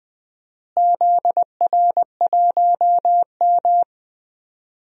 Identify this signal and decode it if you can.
Morse code, heard as a keyed tone, about 21 wpm, 700 Hz. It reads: ZR1M
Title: DXCC-100 Call Signs - 4 Characters - Repeated 3x - 20wpm